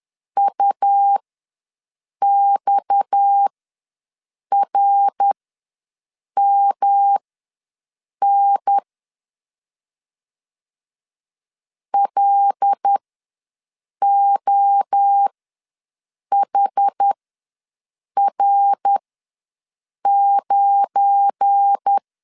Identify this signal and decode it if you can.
Morse code, heard as a keyed tone, about 10 wpm, spaced as for 6 wpm, 800 Hz. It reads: UXRMN LOHR9
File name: LEZIONI / 7 GF9Z / 7 GF9 ws 10wpm 4034